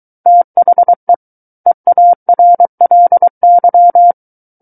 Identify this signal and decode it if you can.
Morse code, heard as a keyed tone, about 23 wpm, 700 Hz. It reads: THE EARLY
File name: Book One / 14-Chapter 14 - 23 WPM